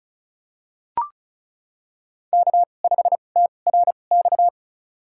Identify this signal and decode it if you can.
Morse code, heard as a keyed tone, about 36 wpm, spaced as for 25 wpm, 700 Hz. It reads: K5TRX
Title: USA Call Signs - 5 Characters - 2x Character Spacing - 35wpm